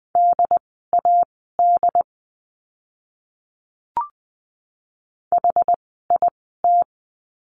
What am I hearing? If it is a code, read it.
Morse code, heard as a keyed tone, about 20 wpm, spaced as for 15 wpm, 700 Hz. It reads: DAD HIT
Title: 068 Lesson 12f - Letter d - Words 20wpm